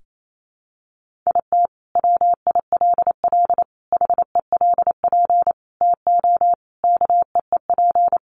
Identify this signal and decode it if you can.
Morse code, heard as a keyed tone, about 28 wpm, 700 Hz. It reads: IT WILL HELP TO KEEP
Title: Sentences from Top 1400 Words 28wpm